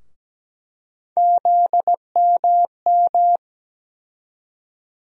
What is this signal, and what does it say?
Morse code, heard as a keyed tone, about 17 wpm, 700 Hz. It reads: ZMM
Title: USA Call Sign Postfixes - 3 Letters 17wpm